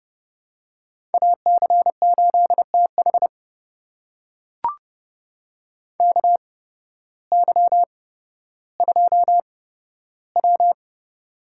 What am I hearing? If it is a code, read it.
Morse code, heard as a keyed tone, about 32 wpm, spaced as for 14 wpm, 700 Hz. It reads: AC8TH KY2W
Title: USA Call Signs - 8x Character Spacing - 30wpm